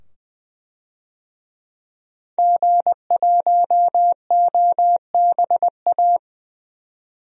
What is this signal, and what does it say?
Morse code, heard as a keyed tone, about 20 wpm, 700 Hz. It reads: G1OBA
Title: DXCC-100 Call Signs - 5 Characters - 20wpm